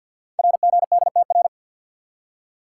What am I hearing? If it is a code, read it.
Morse code, heard as a keyed tone, about 48 wpm, 700 Hz. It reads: RGDTR
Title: Code Groups - 5 Letters - 50wpm